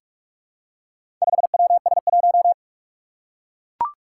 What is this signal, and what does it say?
Morse code, heard as a keyed tone, about 44 wpm, 700 Hz. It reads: 5WS1
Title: Four Challenging Letter-Number Combinations 45wpm